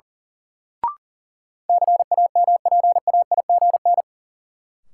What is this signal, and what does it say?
Morse code, heard as a keyed tone, about 38 wpm, 700 Hz. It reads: CAMPAIGN
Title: Sentences from Top 1200 Words - New Word Drill - 40wpm